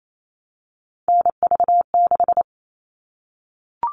Morse code, heard as a keyed tone, about 28 wpm, 700 Hz. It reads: NV6
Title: All Cut Numbers - 3-Digit 28wpm